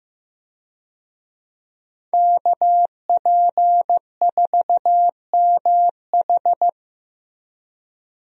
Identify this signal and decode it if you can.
Morse code, heard as a keyed tone, about 15 wpm, 700 Hz. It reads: KP4MH
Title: DXCC-100 Call Signs - 5 Characters - Repeated 2x - 15wpm